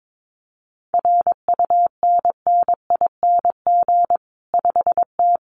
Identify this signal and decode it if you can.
Morse code, heard as a keyed tone, about 22 wpm, 700 Hz. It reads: RUNNING 5T
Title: Common QSO Phrases 22wpm